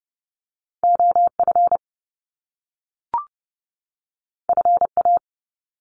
Morse code, heard as a keyed tone, about 30 wpm, 700 Hz. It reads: OF FA
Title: Top Two Letter Combinations in Words 30wpm